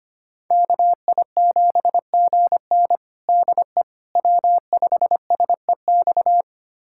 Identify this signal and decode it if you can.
Morse code, heard as a keyed tone, about 25 wpm, 700 Hz. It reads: KI7GN DE W5SEX
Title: USA Callsign DE Callsign - 5 Characters - 25wpm